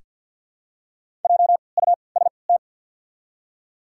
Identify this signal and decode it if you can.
Morse code, heard as a keyed tone, about 48 wpm, spaced as for 28 wpm, 690 Hz. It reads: JUST
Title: 3x Character Spacing - Top 500 Words - 50wpm